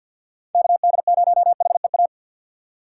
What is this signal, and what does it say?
Morse code, heard as a keyed tone, about 50 wpm, 700 Hz. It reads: KD0HEA
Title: USA Call Signs - 6 Characters - 6x Character Spacing - 50wpm